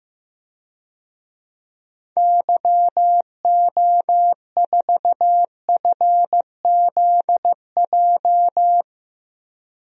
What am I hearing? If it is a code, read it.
Morse code, heard as a keyed tone, about 15 wpm, 700 Hz. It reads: YO4FZJ
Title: DXCC-100 Call Signs - 6 Characters - 15wpm